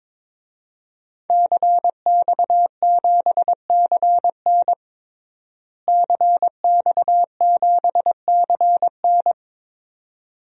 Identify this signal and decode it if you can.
Morse code, heard as a keyed tone, about 22 wpm, 700 Hz. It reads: CX7CN CX7CN
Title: DXCC-100 Call Signs - 5 Characters - Repeated 2x - 22wpm